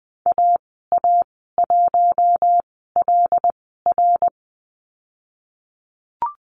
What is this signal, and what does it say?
Morse code, heard as a keyed tone, about 20 wpm, spaced as for 15 wpm, 700 Hz. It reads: AA1LR
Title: 076 Lesson 13g - Letter L - Call Signs 20wpm